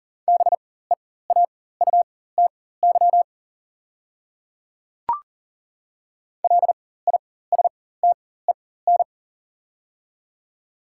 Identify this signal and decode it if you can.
Morse code, heard as a keyed tone, about 38 wpm, spaced as for 18 wpm, 700 Hz. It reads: BEAUTY LISTEN
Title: 4x Character Spacing - Top 500 Words - 40wpm